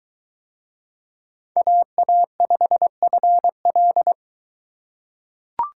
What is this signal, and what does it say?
Morse code, heard as a keyed tone, about 22 wpm, 700 Hz. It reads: AA5FL